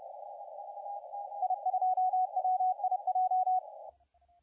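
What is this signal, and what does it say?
Morse code, heard as a keyed tone, about 30 wpm, 720 Hz, with radio noise. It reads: I2WIJ
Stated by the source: CW contest audio recorded at LA4C during the SAC CW contest.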